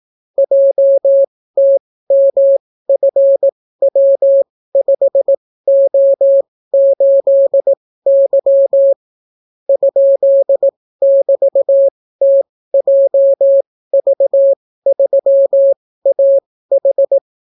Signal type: Morse code